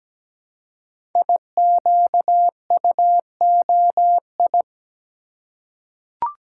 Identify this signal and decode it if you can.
Morse code, heard as a keyed tone, about 17 wpm, 700 Hz. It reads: IQUOI
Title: Code Groups - 5 Letters - 17wpm